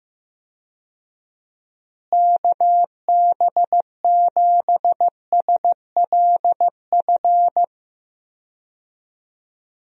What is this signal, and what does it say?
Morse code, heard as a keyed tone, about 15 wpm, 700 Hz. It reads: KB7SLF
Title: USA Call Signs - 6 Characters - Repeated 2x - 15wpm